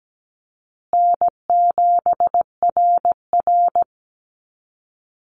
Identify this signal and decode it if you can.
Morse code, heard as a keyed tone, about 17 wpm, 700 Hz. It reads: N7RR
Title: Call Signs Repeated 2x - 17wpm